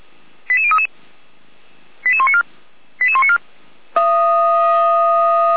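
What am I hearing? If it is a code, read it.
Five-tone selective call (ZVEI), heard as quick runs of five tones, tones of about 70 ms. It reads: 88033 88194 88194
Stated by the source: ZVEI five-tone selective call received on SDR, from Feuerwehr Orlamünde fire brigade audio server.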